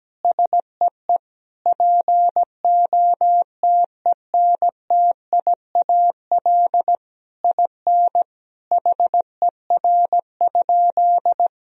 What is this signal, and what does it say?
Morse code, heard as a keyed tone, about 17 wpm, 700 Hz. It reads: SEE POTENTIAL IN HER?
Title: Sentences from Top 1000 Words - Review Part 4 - 17wpm